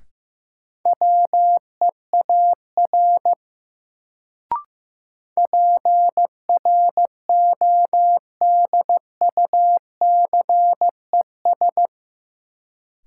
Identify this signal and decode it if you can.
Morse code, heard as a keyed tone, about 15 wpm, 700 Hz. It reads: WEAR PRODUCES
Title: Sentences from Top 1000 Words - Review Part 6 - New Word Drill - 15wpm